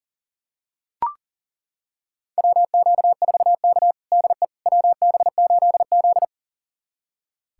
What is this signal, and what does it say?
Morse code, heard as a keyed tone, about 40 wpm, 700 Hz. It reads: WQ4K DE WB8Z
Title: USA Callsign DE Callsign - 4 Characters - 40wpm